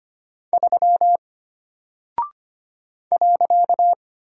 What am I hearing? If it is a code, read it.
Morse code, heard as a keyed tone, about 25 wpm, 700 Hz. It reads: 3 .
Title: Single Character (Letter:Number:Punctuation:Prosign) Alt 25wpm